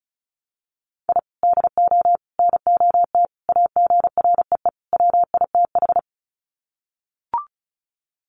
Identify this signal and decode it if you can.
Morse code, heard as a keyed tone, about 34 wpm, 700 Hz. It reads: I DO NOT AGREE WITH
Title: Sets of 5 Words Encore 35wpm